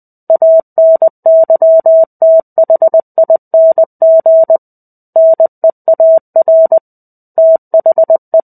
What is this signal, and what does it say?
Morse code, heard as a keyed tone, about 20 wpm, 650 Hz. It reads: ANYTHING NEAR THE
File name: 05-Chapter-05 20 WPM